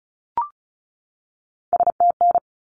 Morse code, heard as a keyed tone, about 34 wpm, 700 Hz. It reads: STN
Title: Abbreviations 35wpm